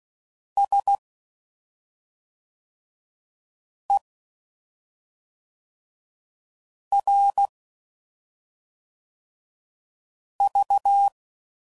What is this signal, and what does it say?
Morse code, heard as a keyed tone, about 16 wpm, 800 Hz. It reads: S E R V
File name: Suffixes / Special ING's / SlowToFaster ING's